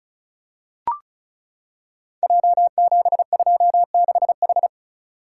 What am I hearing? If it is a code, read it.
Morse code, heard as a keyed tone, about 36 wpm, 700 Hz. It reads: J726H
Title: Code Groups - 5 Letter:Numbers - 35wpm